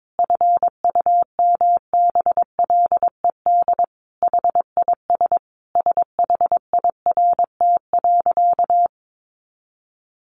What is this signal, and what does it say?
Morse code, heard as a keyed tone, about 22 wpm, 700 Hz. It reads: FUMBLED HIS SHIRT.
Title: A Sound Of Thunder 22wpm